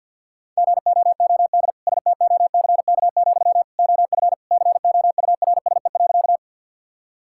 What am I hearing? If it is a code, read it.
Morse code, heard as a keyed tone, about 52 wpm, 700 Hz. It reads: GOOD STOCK, OF COURSE.
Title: Sentences from Top 1000 Words - Review Part 5 - 50wpm